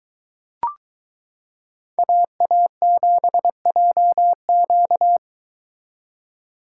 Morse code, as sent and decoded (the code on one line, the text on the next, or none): .- .- --... .--- --.-
AA7JQ